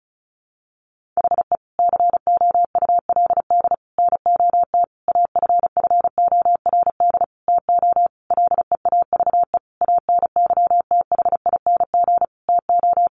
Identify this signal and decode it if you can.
Morse code, heard as a keyed tone, about 34 wpm, 700 Hz. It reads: HE COULD NOT AFFORD TO LEAVE ANYTHING TO
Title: Sentences from Top 1100 Words 35wpm